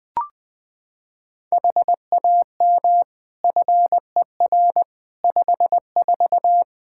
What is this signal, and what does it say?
Morse code, heard as a keyed tone, about 20 wpm, 700 Hz. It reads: HAM FER 54
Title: K1USN SST QSO Practice 20wpm